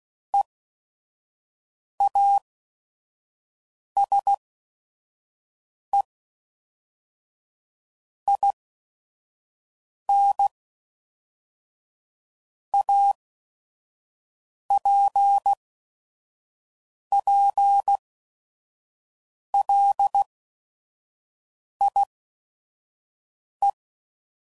Morse code, heard as a keyed tone, about 16 wpm, 800 Hz. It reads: E A S E I N A P P L I E